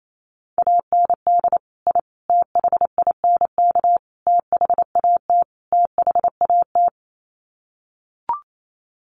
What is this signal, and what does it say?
Morse code, heard as a keyed tone, about 28 wpm, 700 Hz. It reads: AND I THINK THAT THAT
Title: Sets of 5 Words Encore 28wpm